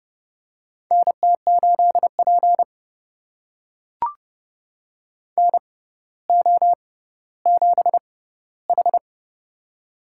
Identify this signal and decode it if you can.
Morse code, heard as a keyed tone, about 29 wpm, spaced as for 15 wpm, 700 Hz. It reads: NT8P NO7H
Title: USA Call Signs - 4 Characters - 6x Character Spacing - 30wpm